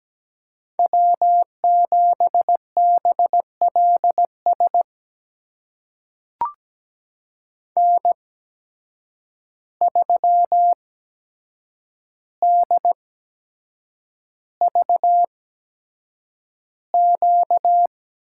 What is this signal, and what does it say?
Morse code, heard as a keyed tone, about 17 wpm, spaced as for 7 wpm, 700 Hz. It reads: W7BLS N3DVQ
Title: USA Call Signs - 5 Characters - 8x Character Spacing - 17wpm